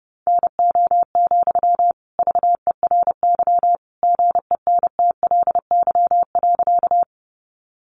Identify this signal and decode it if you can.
Morse code, heard as a keyed tone, about 30 wpm, 700 Hz. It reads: NO, VERY GENTLY.
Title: Sentences from Top 1500 Words 30wpm